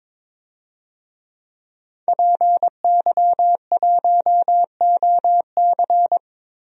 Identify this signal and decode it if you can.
Morse code, heard as a keyed tone, about 22 wpm, 700 Hz. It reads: PY1OC
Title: DXCC-100 Call Signs - 5 Characters - 22wpm